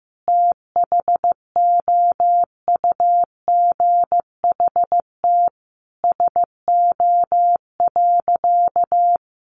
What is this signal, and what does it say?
Morse code, heard as a keyed tone, about 15 wpm, 700 Hz. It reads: THOUGHT SO.
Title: Sentences from Top 1400 Words 15wpm